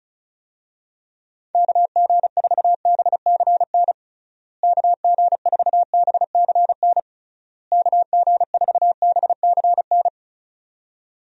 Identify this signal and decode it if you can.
Morse code, heard as a keyed tone, about 34 wpm, 700 Hz. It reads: KG4BCN KG4BCN KG4BCN
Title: USA Call Signs - 6 Characters - Repeated 3x - 35wpm